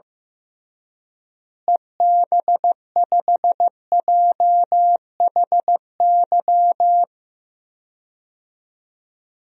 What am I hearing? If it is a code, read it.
Morse code, heard as a keyed tone, about 15 wpm, 700 Hz. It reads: EB5JHY
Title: DXCC-100 Call Signs - 6 Characters - Repeated 3x - 15wpm